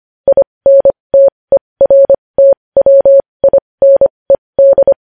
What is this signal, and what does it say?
Morse code, heard as a keyed tone, about 25 wpm, spaced as for 20 wpm, 550 Hz. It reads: INTERTWINED
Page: News Headlines in Morse Code at 20 WPM